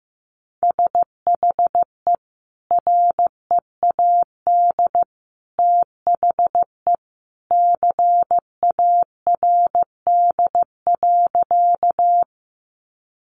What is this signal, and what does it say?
Morse code, heard as a keyed tone, about 15 wpm, 700 Hz. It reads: SHE READ THE CARD.
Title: Sentences from Top 600 Words 15wpm